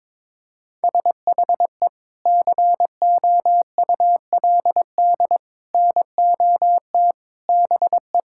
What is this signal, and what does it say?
Morse code, heard as a keyed tone, about 23 wpm, 700 Hz. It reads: SHE COULD NOT BE
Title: Sentences from Top 1000 Words - Review Part 6 - 22wpm